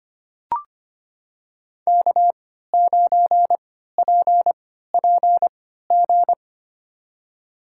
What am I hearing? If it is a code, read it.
Morse code, heard as a keyed tone, about 25 wpm, spaced as for 14 wpm, 700 Hz. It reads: K9PPG
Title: USA Call Signs - 3x Character Spacing - 25wpm